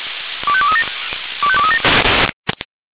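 Five-tone selective call (ZVEI), heard as quick runs of five tones, tones of about 70 ms. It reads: 25237 25237
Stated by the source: ZVEI five-tone selective call received on an SDR.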